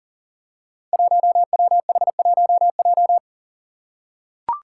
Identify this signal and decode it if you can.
Morse code, heard as a keyed tone, about 38 wpm, 700 Hz. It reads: 1WH1J
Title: Five Challenging Letter-Number Combinations 40wpm